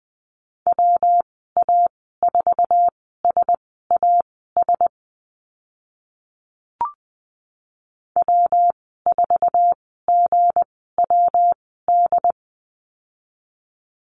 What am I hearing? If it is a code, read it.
Morse code, heard as a keyed tone, about 20 wpm, spaced as for 14 wpm, 700 Hz. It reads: WA4SAS W4GWD
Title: 193 Lesson 31g - Letter v - Call Signs 20wpm